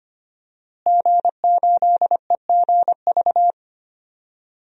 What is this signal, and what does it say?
Morse code, heard as a keyed tone, about 25 wpm, 700 Hz. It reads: G8EGV
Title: DXCC-100 Call Signs - 5 Characters - Repeated 2x - 25wpm